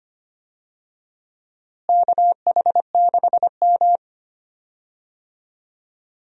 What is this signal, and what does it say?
Morse code, heard as a keyed tone, about 25 wpm, 700 Hz. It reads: KH6M
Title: USA Call Signs - 4 Characters - Repeated 2x - 25wpm